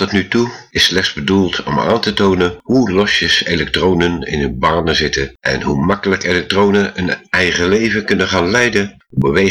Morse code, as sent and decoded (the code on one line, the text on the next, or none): none